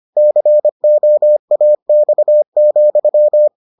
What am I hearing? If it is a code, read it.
Morse code, heard as a keyed tone, about 25 wpm, 600 Hz. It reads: COAX,